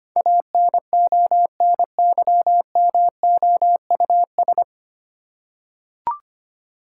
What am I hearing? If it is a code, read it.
Morse code, heard as a keyed tone, about 25 wpm, 700 Hz. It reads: ANONYMOUS
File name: Nine Letter Words - 25wpm